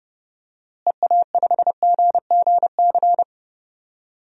Morse code, heard as a keyed tone, about 30 wpm, 700 Hz. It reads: EA5GGC
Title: DXCC-100 Call Signs - 6 Characters - Repeated 2x - 30wpm